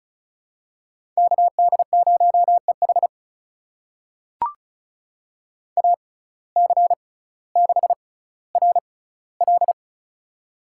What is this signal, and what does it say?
Morse code, heard as a keyed tone, about 36 wpm, spaced as for 18 wpm, 700 Hz. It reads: KD0EH AC6RL
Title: USA Call Signs - 5 Characters - 6x Character Spacing - 35wpm